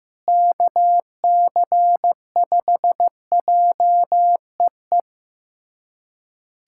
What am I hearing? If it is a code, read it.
Morse code, heard as a keyed tone, about 15 wpm, 700 Hz. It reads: KC5JEE
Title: USA Callsign DE Callsign - 6 Characters - 15wpm